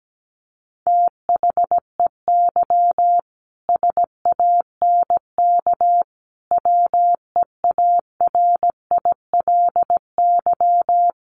Morse code, heard as a keyed tone, about 17 wpm, 700 Hz. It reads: THEY SANK WEARILY
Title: A Sound Of Thunder 17wpm